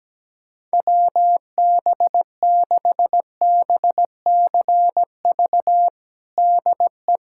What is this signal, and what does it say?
Morse code, heard as a keyed tone, about 17 wpm, 700 Hz. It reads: WB6BCV DE
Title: USA Callsign DE Callsign - 6 Characters - 17wpm